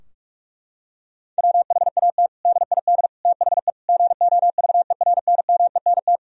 Morse code, heard as a keyed tone, about 46 wpm, 700 Hz. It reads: WHAT DID THE GOVERNMENT
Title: Sentences from Top 1000 Words - Review Part 3 - 45wpm